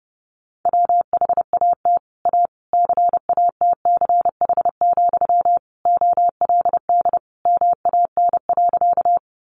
Morse code, heard as a keyed tone, about 30 wpm, 700 Hz. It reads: WHAT A CATCH, OLD MAN.
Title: Sentences from Top 1500 Words 30wpm